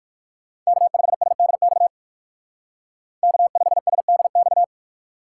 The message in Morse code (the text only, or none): K5SDX K5SDX